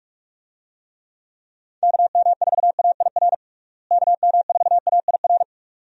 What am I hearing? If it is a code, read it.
Morse code, heard as a keyed tone, about 46 wpm, 700 Hz. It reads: KM4AIR KM4AIR
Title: USA Call Signs - 6 Characters - Repeated 2x - 45wpm